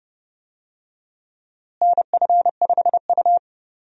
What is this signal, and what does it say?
Morse code, heard as a keyed tone, about 30 wpm, 700 Hz. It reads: NF5U
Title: USA Call Signs - 4 Characters - Repeated 3x - 30wpm